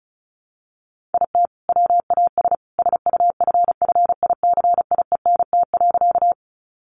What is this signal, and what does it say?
Morse code, heard as a keyed tone, about 34 wpm, 700 Hz. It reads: IT WAS SUFFICIENT.
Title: Sentences from Top 1400 Words 35wpm